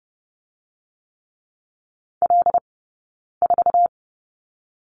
Morse code, heard as a keyed tone, about 32 wpm, 700 Hz. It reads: L 4
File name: Lesson 20d - Number 3 - Two Characters 30wpm